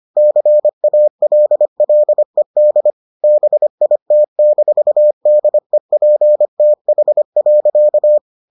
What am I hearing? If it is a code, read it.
Morse code, heard as a keyed tone, about 25 wpm, 600 Hz. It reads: CALLED BIT-DEPTH.